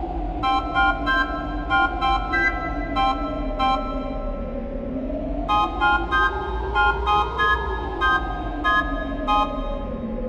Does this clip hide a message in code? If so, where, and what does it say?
No.